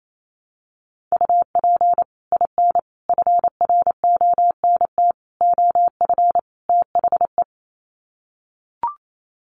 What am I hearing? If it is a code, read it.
Morse code, heard as a keyed tone, about 28 wpm, 700 Hz. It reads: UP IN FRONT OF THE
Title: Sets of 5 Words Encore2 28wpm